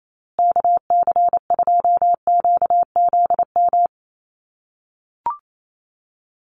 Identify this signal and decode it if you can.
Morse code, heard as a keyed tone, about 28 wpm, 700 Hz. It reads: KC2QZM
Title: USA Call Signs Repeated 2x 28wpm